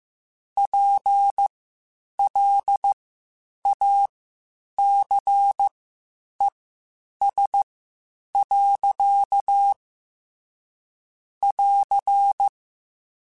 Morse code, heard as a keyed tone, about 15 wpm, spaced as for 8 wpm, 800 Hz. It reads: PLACES. <AR>
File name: Spaced / Chuckles Spaced-mp3s / MiddleAge sp @13